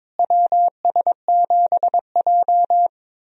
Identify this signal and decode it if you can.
Morse code, heard as a keyed tone, about 21 wpm, 700 Hz. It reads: WS7J
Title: USA Call Signs - 4 Characters - Repeated 3x - 22wpm